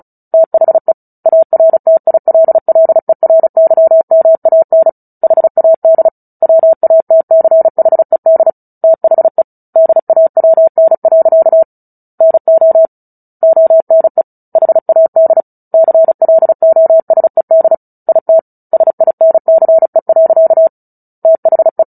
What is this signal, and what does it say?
Morse code, heard as a keyed tone, about 35 wpm, 650 Hz. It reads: THE ARTILLERYMAN HAD WATCHED THE DAWN. NO ONE HAD CLOSED IT SINCE. THE
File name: Book Two / 09-Chapter 9 - 35 WPM